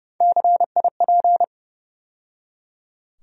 Morse code, heard as a keyed tone, about 31 wpm, 700 Hz. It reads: CIP
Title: USA Call Sign Postfixes - 3 Letters 30wpm